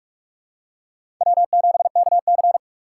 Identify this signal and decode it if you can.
Morse code, heard as a keyed tone, about 44 wpm, 700 Hz. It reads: W7KC